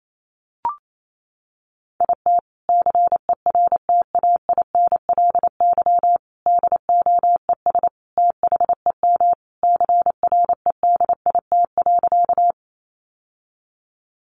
Morse code, heard as a keyed tone, about 28 wpm, 700 Hz. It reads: IT CERTAINLY DOES THEM CREDIT.
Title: Sentences from Top 1000 Words 28wpm